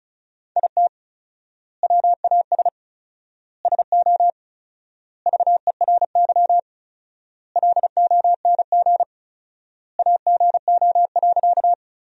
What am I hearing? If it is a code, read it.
Morse code, heard as a keyed tone, about 36 wpm, 700 Hz. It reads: IT WAS SO VERY LONG AGO.
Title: 4x Word-Spacing - Sentences from Top 500 Words - 35wpm